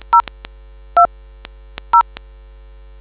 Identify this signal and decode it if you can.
DTMF, heard as telephone touch tones, tones of about 100 ms. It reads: 02 0